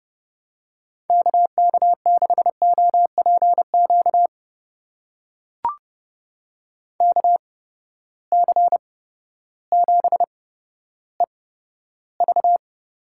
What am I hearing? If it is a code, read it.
Morse code, heard as a keyed tone, about 30 wpm, spaced as for 14 wpm, 700 Hz. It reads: KK6OPQ KC7EV
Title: USA Call Signs - 6 Characters - 8x Character Spacing - 30wpm